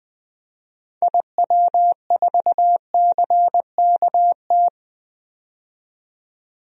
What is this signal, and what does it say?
Morse code, heard as a keyed tone, about 20 wpm, 700 Hz. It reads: IW4CKT
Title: DXCC-100 Call Signs - 6 Characters - 20wpm